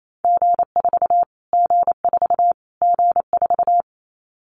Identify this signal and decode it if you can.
Morse code, heard as a keyed tone, about 28 wpm, 700 Hz. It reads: G4 G4 G4
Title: Callsign Prefixes for Top 25 DXCC - Repeated 3x - 28wpm